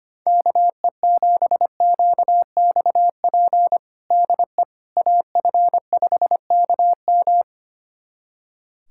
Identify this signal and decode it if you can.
Morse code, heard as a keyed tone, about 25 wpm, 700 Hz. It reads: KE7QXP DE AF5KM
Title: USA Callsign DE Callsign 25wpm